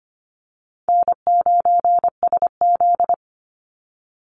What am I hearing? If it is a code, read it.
Morse code, heard as a keyed tone, about 25 wpm, 700 Hz. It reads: N9SZ
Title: Easy Call Signs Repeated 3x - 25wpm